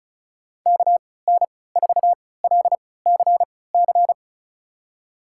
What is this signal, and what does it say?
Morse code, heard as a keyed tone, about 36 wpm, spaced as for 20 wpm, 700 Hz. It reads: KN4LCC
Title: USA Call Signs - 3x Character Spacing - 35wpm